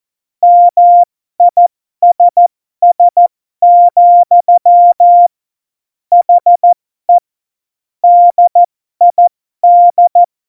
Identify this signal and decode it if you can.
Morse code, heard as a keyed tone, about 14 wpm, spaced as for 12 wpm, 700 Hz. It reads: MISS, HE DID
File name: Morsekurs / Witze